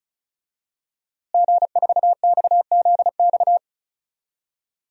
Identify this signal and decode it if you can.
Morse code, heard as a keyed tone, about 36 wpm, 700 Hz. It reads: G4XZX
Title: DXCC-100 Call Signs - 5 Characters - Repeated 3x - 35wpm